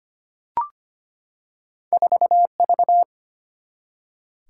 Morse code, heard as a keyed tone, about 25 wpm, 700 Hz. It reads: 4V
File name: Two Letter-Number Combinations - 25wpm